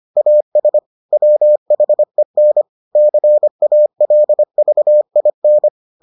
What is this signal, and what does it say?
Morse code, heard as a keyed tone, about 25 wpm, 600 Hz. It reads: AS WHEN CALVIN